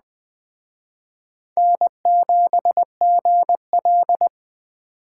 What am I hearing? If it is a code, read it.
Morse code, heard as a keyed tone, about 20 wpm, 700 Hz. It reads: N7GL
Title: Speed Racing - USA 1x2 Call Signs 17wpm